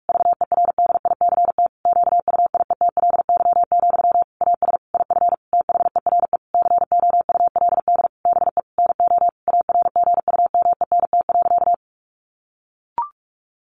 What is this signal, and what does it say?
Morse code, heard as a keyed tone, about 44 wpm, 700 Hz. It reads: VERDICT QUIETLY, AS IF THERE COULD BE NO ARGUMENT.
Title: A Sound Of Thunder 45wpm